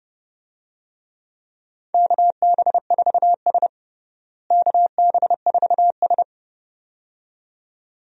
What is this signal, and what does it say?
Morse code, heard as a keyed tone, about 30 wpm, 700 Hz. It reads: KB4S KB4S
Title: USA Call Signs - 4 Characters - Repeated 2x - 30wpm